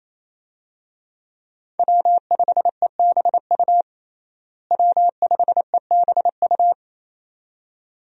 Morse code, heard as a keyed tone, about 28 wpm, 700 Hz. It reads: W5EBU W5EBU
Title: USA Call Signs - 5 Characters - Repeated 2x - 28wpm